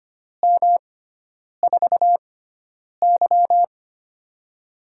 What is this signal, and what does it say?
Morse code, heard as a keyed tone, about 25 wpm, spaced as for 9 wpm, 700 Hz. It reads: M4Y
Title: USA Call Signs - 4 Characters - 6x Character Spacing - 25wpm